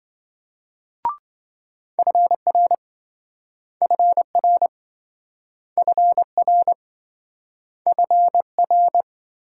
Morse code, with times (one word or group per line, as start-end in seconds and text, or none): 2.0-2.7 FR
3.8-4.7 FR
5.8-6.7 FR
7.9-9.0 FR